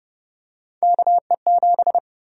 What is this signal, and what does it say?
Morse code, heard as a keyed tone, about 30 wpm, 700 Hz. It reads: KE7
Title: USA Call Sign Prefixes - 30wpm